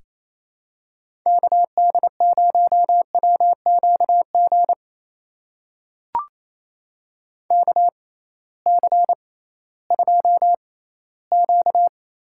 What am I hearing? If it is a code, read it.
Morse code, heard as a keyed tone, about 28 wpm, spaced as for 16 wpm, 700 Hz. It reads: KD0WQG KC2Q
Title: USA Call Signs - 6 Characters - 6x Character Spacing - 28wpm